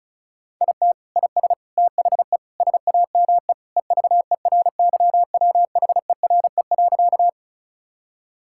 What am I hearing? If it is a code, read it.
Morse code, heard as a keyed tone, about 34 wpm, 700 Hz. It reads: IT IS THE SAME EVERYWHERE.
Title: Sentences from Top 1600 Words 35wpm